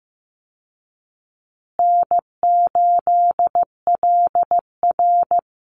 Morse code, heard as a keyed tone, about 15 wpm, 700 Hz. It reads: N8LR
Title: Easy Call Signs Repeated 3x - 15wpm